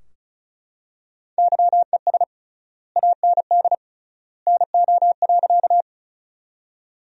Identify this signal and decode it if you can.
Morse code, heard as a keyed tone, about 34 wpm, 700 Hz. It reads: YES AND NO.
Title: 3x Word-Spacing - Sentences from Top 500 Words - 35wpm